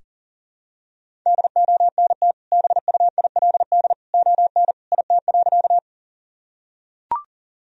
Morse code, heard as a keyed tone, about 40 wpm, 700 Hz. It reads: DONT BUILD ON IT.
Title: Sentences from Top 1000 Words - Review Part 1 - 40wpm